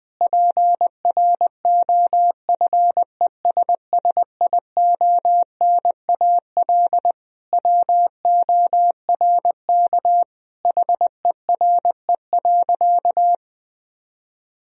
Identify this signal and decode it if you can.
Morse code, heard as a keyed tone, about 20 wpm, 700 Hz. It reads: PROFESSIONAL WORK HERE.
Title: Sentences from Top 1000 Words - Review Part 4 - 20wpm